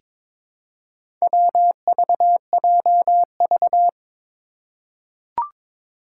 Morse code, heard as a keyed tone, about 22 wpm, 700 Hz. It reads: WVJV
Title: Four Challenging Letter-Number Combinations 22wpm